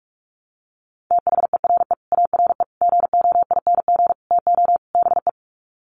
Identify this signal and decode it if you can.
Morse code, heard as a keyed tone, about 44 wpm, 700 Hz. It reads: THERE ARE GOING TO BE
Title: Sets of 5 Words 45wpm